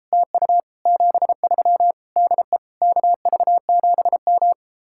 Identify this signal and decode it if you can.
Morse code, heard as a keyed tone, about 35 wpm, 700 Hz. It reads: TU 73 DE KV7M
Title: Speed-Racing Common QSO Phrases - 28wpm